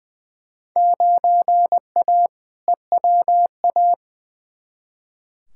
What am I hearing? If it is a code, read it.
Morse code, heard as a keyed tone, about 20 wpm, 700 Hz. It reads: 9A EWA
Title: Field Day Exchanges Encore - 20wpm